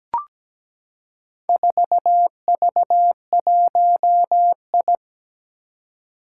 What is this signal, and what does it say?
Morse code, heard as a keyed tone, about 17 wpm, 700 Hz. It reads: 4V1I
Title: Four Challenging Letter-Number Combinations 17wpm